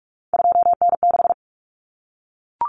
Morse code, heard as a keyed tone, about 44 wpm, 700 Hz. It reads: 2N6
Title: Common Cut Numbers - 3-Digit 45wpm